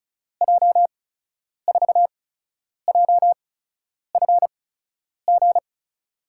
Morse code, heard as a keyed tone, about 36 wpm, spaced as for 10 wpm, 700 Hz. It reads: J4JFG
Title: USA Call Signs - 8x Character Spacing - 35wpm